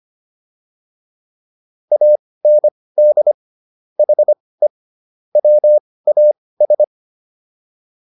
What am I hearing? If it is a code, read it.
Morse code, heard as a keyed tone, about 25 wpm, spaced as for 18 wpm, 600 Hz. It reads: AND HE WAS
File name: WD204 18